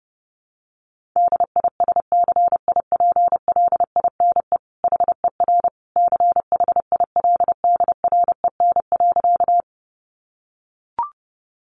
Morse code, heard as a keyed tone, about 30 wpm, 700 Hz. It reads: DISCIPLINE HER CHILDREN.
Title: Sentences from Top 900 Words 30wpm